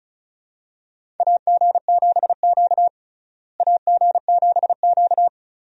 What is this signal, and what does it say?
Morse code, heard as a keyed tone, about 36 wpm, 700 Hz. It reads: AG7Q AG7Q
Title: USA Call Signs - 4 Characters - Repeated 2x - 35wpm